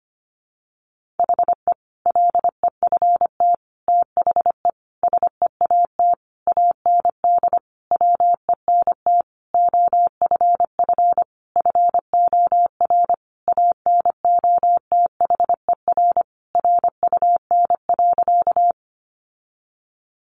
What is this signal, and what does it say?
Morse code, heard as a keyed tone, about 25 wpm, 700 Hz. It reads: HE LEFT THE SEAT AND WENT OFF FOR ANOTHER RUN.
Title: Sentences from Top 1500 Words 25wpm